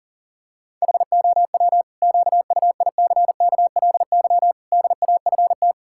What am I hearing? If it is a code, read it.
Morse code, heard as a keyed tone, about 40 wpm, 700 Hz. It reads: HOW QUICKLY DAFT
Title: Pangram Challenge v2 - 40wpm